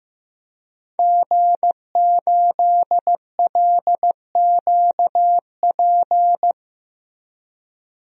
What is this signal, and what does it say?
Morse code, heard as a keyed tone, about 15 wpm, 700 Hz. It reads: G8LQP
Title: DXCC-100 Call Signs - 5 Characters - Repeated 2x - 15wpm